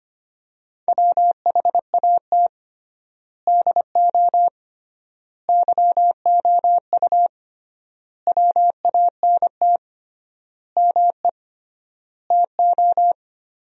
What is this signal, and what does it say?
Morse code, heard as a keyed tone, about 25 wpm, 700 Hz. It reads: WHAT DO YOU WANT ME TO
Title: 3x Word-Spacing - Sentences from Top 200 Words - 25wpm